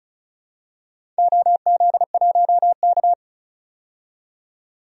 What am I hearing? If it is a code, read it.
Morse code, heard as a keyed tone, about 36 wpm, 700 Hz. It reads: OZ1K
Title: DXCC-100 Call Signs - 4 Characters - 35wpm